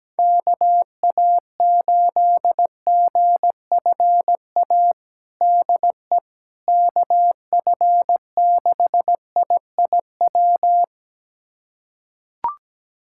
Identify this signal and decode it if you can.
Morse code, heard as a keyed tone, about 17 wpm, 700 Hz. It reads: KA8GFA DE KF6IIW
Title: USA Callsign DE Callsign 17wpm